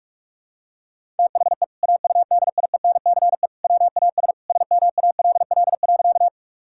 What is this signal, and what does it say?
Morse code, heard as a keyed tone, about 46 wpm, 700 Hz. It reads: THE AUDIENCE WAS SMALL.
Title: Sentences from Top 1000 Words - Review Part 5 - 45wpm